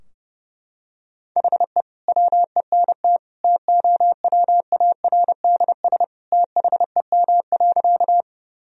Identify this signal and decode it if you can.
Morse code, heard as a keyed tone, about 30 wpm, 700 Hz. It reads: HE WENT TOWARDS THEM.
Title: Sentences from Top 1000 Words - Review Part 2 - 30wpm